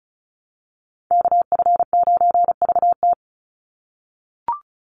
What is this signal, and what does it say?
Morse code, heard as a keyed tone, about 34 wpm, 700 Hz. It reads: KF9VT
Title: USA Call Signs 35wpm